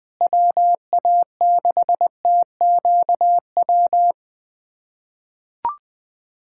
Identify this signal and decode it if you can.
Morse code, heard as a keyed tone, about 20 wpm, 700 Hz. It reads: WA6TQW
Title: USA Callsign DE Callsign - 6 Characters - 20wpm